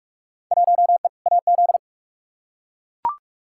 Morse code, heard as a keyed tone, about 43 wpm, 700 Hz. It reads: 1E AZ
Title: Field Day Exchanges - 45wpm